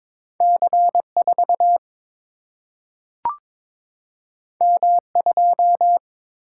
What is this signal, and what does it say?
Morse code, heard as a keyed tone, about 22 wpm, 700 Hz. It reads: C4 M2
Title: Single-Letter Single-Number 22wpm